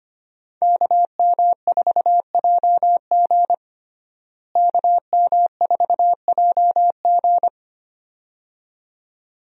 Morse code, heard as a keyed tone, about 25 wpm, 700 Hz. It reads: KM4JG KM4JG
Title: USA Call Signs - 5 Characters - Repeated 2x - 25wpm